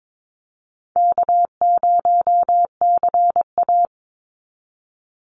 Morse code, heard as a keyed tone, about 22 wpm, 700 Hz. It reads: K0CA
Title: Call Signs Repeated 2x - 22wpm